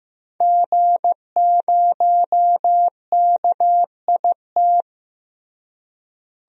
Morse code, heard as a keyed tone, about 15 wpm, 700 Hz. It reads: G0KIT
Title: DXCC-100 Call Signs - 5 Characters - Repeated 2x - 15wpm